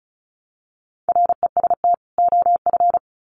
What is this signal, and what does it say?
Morse code, heard as a keyed tone, about 34 wpm, 700 Hz. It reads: REST OF
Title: Sets of 2 Words 35wpm